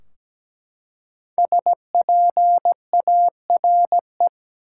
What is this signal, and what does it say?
Morse code, heard as a keyed tone, about 17 wpm, 700 Hz. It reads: SPARE
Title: Five Letter Words - 17wpm